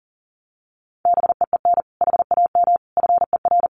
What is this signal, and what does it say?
Morse code, heard as a keyed tone, about 40 wpm, 700 Hz. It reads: BEEN HAM FER
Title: Common QSO Phrases Encore 40wpm